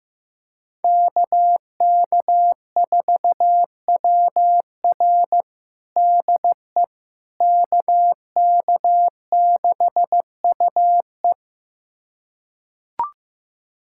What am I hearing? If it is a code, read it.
Morse code, heard as a keyed tone, about 15 wpm, 700 Hz. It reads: KK4WR DE KK6UE
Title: USA Callsign DE Callsign - 5 Characters - 15wpm